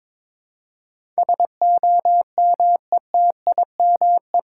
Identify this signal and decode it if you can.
Morse code, heard as a keyed tone, about 21 wpm, 700 Hz. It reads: SOMETIME
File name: practice / 22wpm / Eight Letter Words - 22wpm